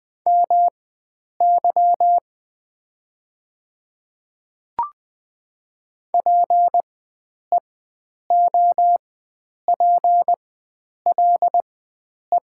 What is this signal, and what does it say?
Morse code, heard as a keyed tone, about 20 wpm, spaced as for 9 wpm, 700 Hz. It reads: MY PEOPLE
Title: 4x Character Spacing - Top 100 Words - 20wpm